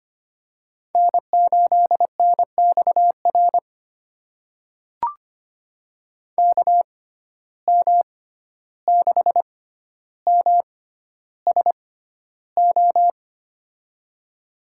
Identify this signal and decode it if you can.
Morse code, heard as a keyed tone, about 25 wpm, spaced as for 12 wpm, 700 Hz. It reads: N8NXR KM6MSO
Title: USA Call Signs - 6x Character Spacing - 25wpm